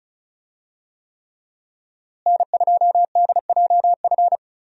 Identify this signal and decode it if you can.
Morse code, heard as a keyed tone, about 34 wpm, 700 Hz. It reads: N2DJF
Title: USA Call Signs - 5 Characters - Repeated 3x - 35wpm